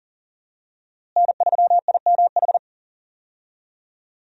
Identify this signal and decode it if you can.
Morse code, heard as a keyed tone, about 40 wpm, 700 Hz. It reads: N3IMH
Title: USA Call Signs - 5 Characters - Repeated 2x - 40wpm